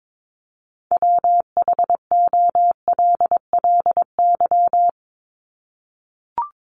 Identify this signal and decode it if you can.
Morse code, heard as a keyed tone, about 22 wpm, 700 Hz. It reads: WHOLLY
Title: LY Suffix - Top 100 Words with Base Words 22wpm